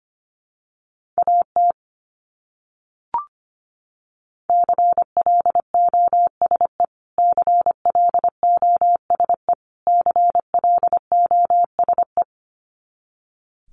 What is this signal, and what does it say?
Morse code, heard as a keyed tone, about 25 wpm, 700 Hz. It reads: AT CLOSE CLOSE CLOSE
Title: Top 500 Words Repeated 25wpm